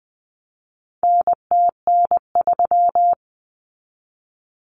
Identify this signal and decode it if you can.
Morse code, heard as a keyed tone, about 20 wpm, 700 Hz. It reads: NTN3
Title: Common Cut Numbers - 4-Digit 20wpm